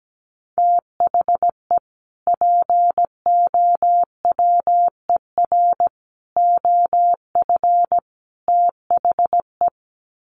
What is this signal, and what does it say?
Morse code, heard as a keyed tone, about 17 wpm, 700 Hz. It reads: THE POWER OF THE
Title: Sets of 4 Words 17wpm